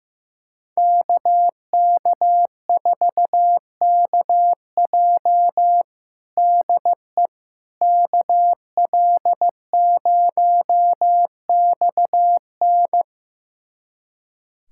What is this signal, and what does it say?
Morse code, heard as a keyed tone, about 16 wpm, 700 Hz. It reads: KK4KJ DE KL0XN
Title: USA Callsign DE Callsign - 5 Characters - 15wpm